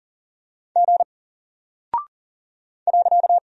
Morse code, heard as a keyed tone, about 40 wpm, 700 Hz. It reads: G .
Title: Single Character (Letter:Number:Punctuation:Prosign) Alt 40wpm